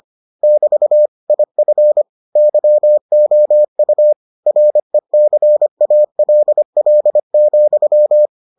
Morse code, heard as a keyed tone, about 25 wpm, 600 Hz. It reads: <BT> IF YOU RECALL,